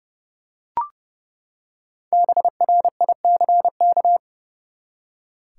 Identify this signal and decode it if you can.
Morse code, heard as a keyed tone, about 30 wpm, 700 Hz. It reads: BRICK
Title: Five Letter Words - 30wpm